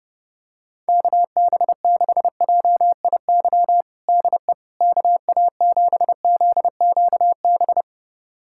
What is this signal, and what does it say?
Morse code, heard as a keyed tone, about 32 wpm, 700 Hz. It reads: KB6JIY DE KA7ZQB
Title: USA Callsign DE Callsign - 6 Characters - 30wpm